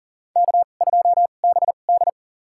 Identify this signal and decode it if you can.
Morse code, heard as a keyed tone, about 40 wpm, spaced as for 29 wpm, 700 Hz. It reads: K2BD
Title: USA Call Signs - 4 Characters - 2x Character Spacing - 40wpm